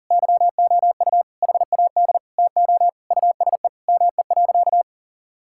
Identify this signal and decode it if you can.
Morse code, heard as a keyed tone, about 40 wpm, 700 Hz. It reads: YOU HAD TO USE ME.
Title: Sentences from Top 1000 Words - Review Part 6 - 40wpm